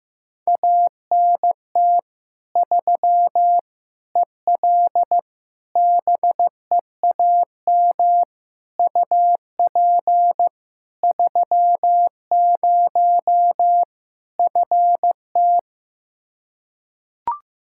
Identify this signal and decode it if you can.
Morse code, heard as a keyed tone, about 16 wpm, 700 Hz. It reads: ANT 3 EL BEAM UP 30 FT
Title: Practice QSO - Exchange 2 - Info - 15wpm